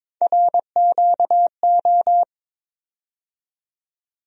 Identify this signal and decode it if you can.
Morse code, heard as a keyed tone, about 22 wpm, 700 Hz. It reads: RQO
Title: USA Call Sign Postfixes - 3 Letters 22wpm